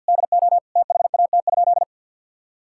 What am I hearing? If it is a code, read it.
Morse code, heard as a keyed tone, about 52 wpm, 700 Hz. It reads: DO THAT?
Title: Sentences from Top 1000 Words - Review Part 6 - 50wpm